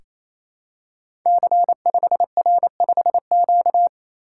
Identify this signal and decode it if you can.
Morse code, heard as a keyed tone, about 27 wpm, 700 Hz. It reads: C5R5Q
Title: Code Groups - 5 Letter:Numbers - 28wpm